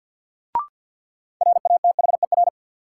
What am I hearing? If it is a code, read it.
Morse code, heard as a keyed tone, about 51 wpm, 700 Hz. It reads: RATHER
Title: Sets of 2 Words - New Word Drill - 50wpm